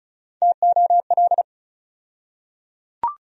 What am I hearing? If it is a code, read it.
Morse code, heard as a keyed tone, about 34 wpm, 700 Hz. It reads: TOL
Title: USA Call Sign Postfixes - 3 Letters 35wpm